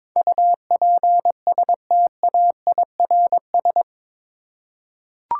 Morse code, heard as a keyed tone, about 22 wpm, 700 Hz. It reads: UPSTAIRS
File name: practice / 22wpm / Eight Letter Words - 22wpm